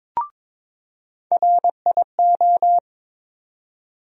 Morse code, heard as a keyed tone, about 22 wpm, 700 Hz. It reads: RIO
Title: USA Call Sign Postfixes - 3 Letters 22wpm